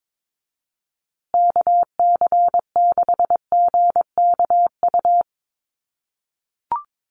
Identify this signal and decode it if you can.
Morse code, heard as a keyed tone, about 22 wpm, 700 Hz. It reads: KC6GKU
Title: USA Call Signs Repeated 3x 22wpm